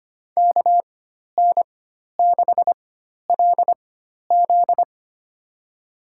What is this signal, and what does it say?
Morse code, heard as a keyed tone, about 25 wpm, spaced as for 12 wpm, 700 Hz. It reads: KN6LZ
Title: USA Call Signs - 4x Character Spacing - 25wpm